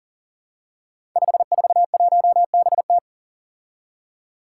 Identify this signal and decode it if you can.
Morse code, heard as a keyed tone, about 40 wpm, 700 Hz. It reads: 541BT
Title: Five Challenging Letter-Number Combinations 40wpm